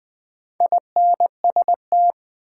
Morse code, heard as a keyed tone, about 21 wpm, 700 Hz. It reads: INST
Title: Build A Word - 2 Letters At A Time - 8-Letter Words - 20wpm